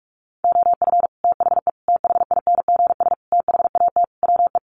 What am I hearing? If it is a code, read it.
Morse code, heard as a keyed tone, about 44 wpm, 700 Hz. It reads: OF THE THINGS THAT WE
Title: Sets of 5 Words Encore 45wpm